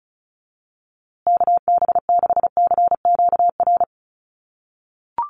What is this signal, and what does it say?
Morse code, heard as a keyed tone, about 34 wpm, 700 Hz. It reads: KB6CQR
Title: USA Call Signs Repeated 3x 35wpm